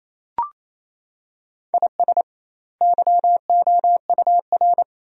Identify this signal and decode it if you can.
Morse code, heard as a keyed tone, about 28 wpm, 700 Hz. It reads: IS YOUR
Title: 2x Word-Spacing - Sentences from Top 400 Words - 28wpm